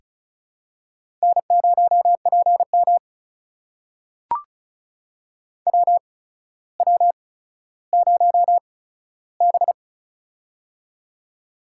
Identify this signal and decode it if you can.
Morse code, heard as a keyed tone, about 36 wpm, spaced as for 15 wpm, 700 Hz. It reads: N0PM WW0B
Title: USA Call Signs - 4 Characters - 8x Character Spacing - 35wpm